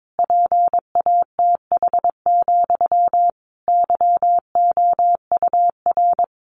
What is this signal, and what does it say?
Morse code, heard as a keyed tone, about 22 wpm, 700 Hz. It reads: PATH, YOUR
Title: A Sound Of Thunder 22wpm